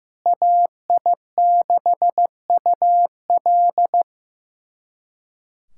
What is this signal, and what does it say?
Morse code, heard as a keyed tone, about 15 wpm, 700 Hz. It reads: AI6UL
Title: USA Callsign DE Callsign - 5 Characters - 15wpm